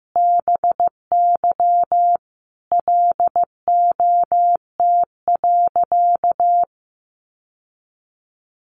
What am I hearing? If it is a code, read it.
Morse code, heard as a keyed tone, about 15 wpm, 700 Hz. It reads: BY LOT.
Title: Sentences from Top 1500 Words 15wpm